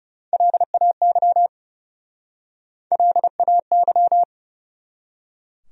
Morse code, heard as a keyed tone, about 41 wpm, 700 Hz. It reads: LAY LAY